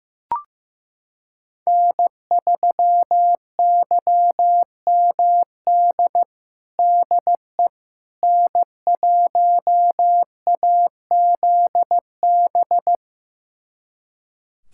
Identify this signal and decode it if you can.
Morse code, heard as a keyed tone, about 15 wpm, 700 Hz. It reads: N3YMD DE N1AZB
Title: USA Callsign DE Callsign - 5 Characters - 15wpm